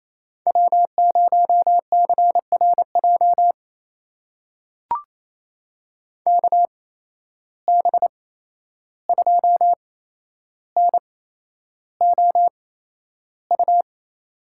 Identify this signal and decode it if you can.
Morse code, heard as a keyed tone, about 28 wpm, spaced as for 11 wpm, 700 Hz. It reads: W0CRJ KB2NOU
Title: USA Call Signs - 8x Character Spacing - 28wpm